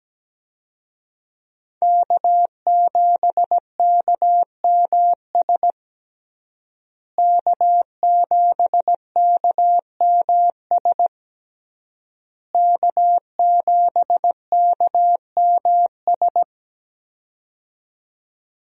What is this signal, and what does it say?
Morse code, heard as a keyed tone, about 17 wpm, 700 Hz. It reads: K7KMS K7KMS K7KMS
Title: USA Call Signs - 5 Characters - Repeated 3x - 17wpm